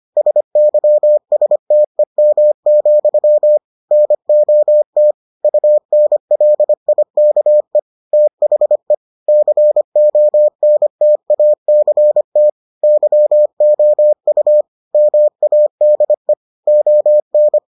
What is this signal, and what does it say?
Morse code, heard as a keyed tone, about 25 wpm, 600 Hz. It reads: SYSTEM, NOT UNLIKE THE CONTACT YOU MADE ON